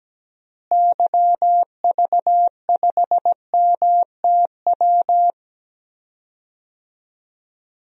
Morse code, heard as a keyed tone, about 17 wpm, 700 Hz. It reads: YV5MTW
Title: DXCC-100 Call Signs - 6 Characters - Repeated 2x - 17wpm